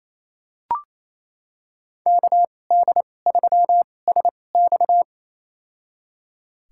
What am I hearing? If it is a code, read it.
Morse code, heard as a keyed tone, about 28 wpm, spaced as for 20 wpm, 700 Hz. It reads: KD3SX
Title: USA Call Signs - 5 Characters - 2x Character Spacing - 28wpm